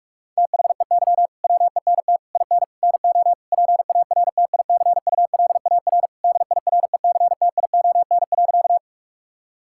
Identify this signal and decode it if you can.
Morse code, heard as a keyed tone, about 46 wpm, 700 Hz. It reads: THEY WENT IN NO PARTICULAR DIRECTION.
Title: Sentences from Top 1000 Words - Review Part 5 - 45wpm